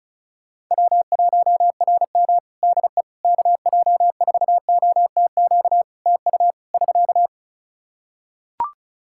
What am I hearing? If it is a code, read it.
Morse code, heard as a keyed tone, about 36 wpm, 700 Hz. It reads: W1RM DE KJ4OTQ TU <SK>
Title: Practice QSO - Exchange 4 - Ending - 35wpm